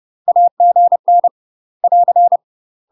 Morse code, heard as a keyed tone, about 30 wpm, 700 Hz. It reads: AGN <AR>
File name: UK MASTER QSOs TRAINING/30WPM